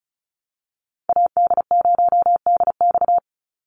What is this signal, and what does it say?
Morse code, heard as a keyed tone, about 34 wpm, 700 Hz. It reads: AD0DX
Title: Call Signs Repeated 2x - 35wpm